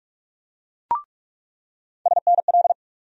Morse code, heard as a keyed tone, about 46 wpm, 700 Hz. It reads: SNL
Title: USA Call Sign Postfixes - 3 Letters 45wpm